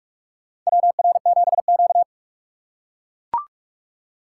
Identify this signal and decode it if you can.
Morse code, heard as a keyed tone, about 44 wpm, 700 Hz. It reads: WR7Q
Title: USA Call Signs - 6x Character Spacing - 45wpm